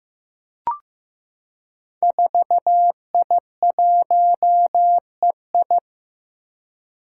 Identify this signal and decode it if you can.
Morse code, heard as a keyed tone, about 15 wpm, 700 Hz. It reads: 4I1EI
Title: Five Challenging Letter-Number Combinations 15wpm